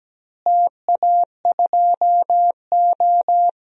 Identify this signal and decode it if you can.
Morse code, heard as a keyed tone, about 17 wpm, 700 Hz. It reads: TA2O
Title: DXCC-100 Call Signs - 4 Characters - Repeated 2x - 17wpm